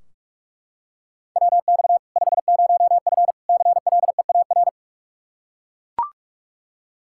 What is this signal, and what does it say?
Morse code, heard as a keyed tone, about 41 wpm, 700 Hz. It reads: WX 50F CLEAR
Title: K1USN SST QSO Practice 45wpm